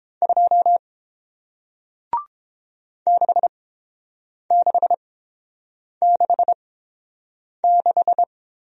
Morse code, heard as a keyed tone, about 30 wpm, spaced as for 7 wpm, 700 Hz. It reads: 2 6666